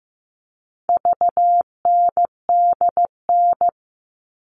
Morse code, heard as a keyed tone, about 15 wpm, 700 Hz. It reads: VNDN
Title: All Cut Numbers - 4-Digit 15wpm